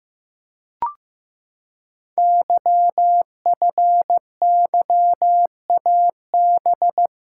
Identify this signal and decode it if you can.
Morse code, heard as a keyed tone, about 15 wpm, 700 Hz. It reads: YFYAB
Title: Code Groups - 5 Letter:Numbers - 15wpm